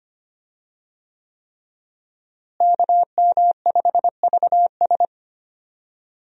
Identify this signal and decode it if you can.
Morse code, heard as a keyed tone, about 25 wpm, 700 Hz. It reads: KM5VS
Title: USA Call Signs - 5 Characters - Repeated 3x - 25wpm